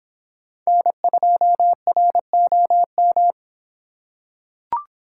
Morse code, heard as a keyed tone, about 26 wpm, 700 Hz. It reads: N2ROM